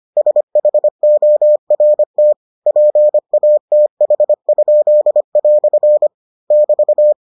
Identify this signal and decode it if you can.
Morse code, heard as a keyed tone, about 25 wpm, 600 Hz. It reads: SHORT PATH?" <BT>